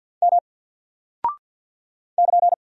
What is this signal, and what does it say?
Morse code, heard as a keyed tone, about 48 wpm, 700 Hz. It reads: M <KN>
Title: Single Character (Letter:Number:Punctuation:Prosign) 50wpm